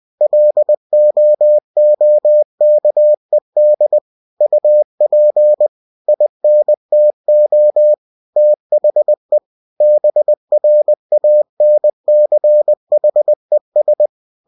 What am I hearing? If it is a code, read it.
Morse code, heard as a keyed tone, about 20 wpm, 600 Hz. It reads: LOOKED UP INTO THE BRANCHES
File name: cw-books / winnie-the-pooh / winnie-20wpm-page-0024